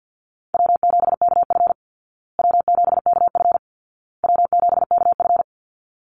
Morse code, heard as a keyed tone, about 49 wpm, 700 Hz. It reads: W7KF W7KF W7KF
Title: Easy Call Signs Repeated 3x - 50wpm